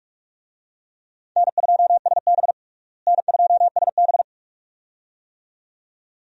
Morse code, heard as a keyed tone, about 46 wpm, 700 Hz. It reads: N2SB N2SB
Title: USA Call Signs - 4 Characters - Repeated 2x - 45wpm